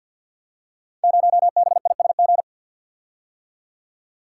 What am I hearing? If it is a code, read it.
Morse code, heard as a keyed tone, about 52 wpm, 690 Hz. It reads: 0BISG
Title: Code Groups - 5 Letter:Numbers - 50wpm